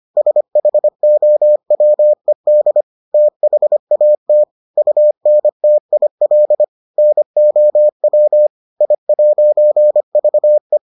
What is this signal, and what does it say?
Morse code, heard as a keyed tone, about 25 wpm, 600 Hz. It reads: SHOWED THAT UNTIL NOW I'VE